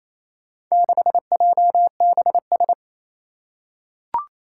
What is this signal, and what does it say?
Morse code, heard as a keyed tone, about 28 wpm, 700 Hz. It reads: 6JBS
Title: Four Challenging Letter-Number Combinations 28wpm